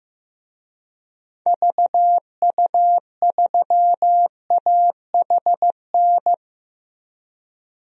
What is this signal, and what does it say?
Morse code, heard as a keyed tone, about 15 wpm, 700 Hz. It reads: VU3AHN
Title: DXCC-100 Call Signs - 6 Characters - 15wpm